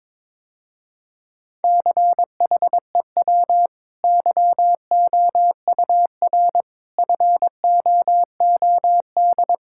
Morse code, heard as a keyed tone, about 21 wpm, 700 Hz. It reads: CHEW YOUR FOOD
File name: practice / 22wpm / Easy Sentences with Hints - 22wpm